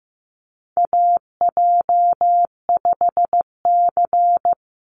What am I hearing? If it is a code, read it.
Morse code, heard as a keyed tone, about 15 wpm, 700 Hz. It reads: AJ5C
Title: NA SOTA Chaser Callsigns 15wpm